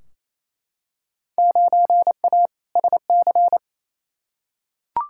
Morse code, heard as a keyed tone, about 28 wpm, 700 Hz. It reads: 9A SC
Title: Field Day Exchanges Encore - 28wpm